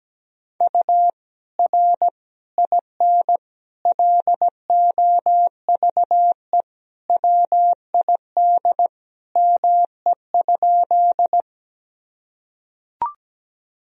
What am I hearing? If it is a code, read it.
Morse code, heard as a keyed tone, about 17 wpm, 700 Hz. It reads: U R IN LOVE WID ME?
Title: Sentences from Top 400 Words and Abbreviations 17wpm